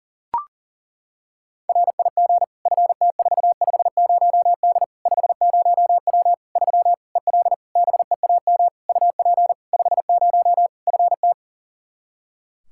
Morse code, heard as a keyed tone, about 37 wpm, 700 Hz. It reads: RIG FT450D 50W 3 EL BEAM UP 50 FT
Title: K1USN SST QSO Practice 40wpm